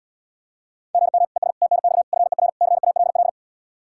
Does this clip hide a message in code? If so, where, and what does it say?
Yes: Morse code from 0.9 to 3.3 s, heard as a keyed tone, about 77 wpm, 700 Hz. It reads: GM ES TNX FER CALL